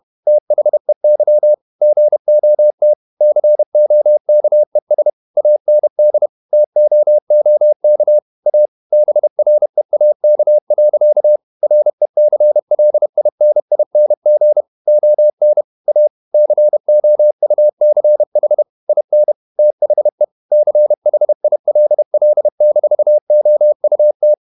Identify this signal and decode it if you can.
Morse code, heard as a keyed tone, about 31 wpm, 600 Hz. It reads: THEY GOT COKES AND TOOK A BREAK. RECLINING ON A COUCH IN THE CHILL-OUT
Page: Never 31wpm | CW med Gnister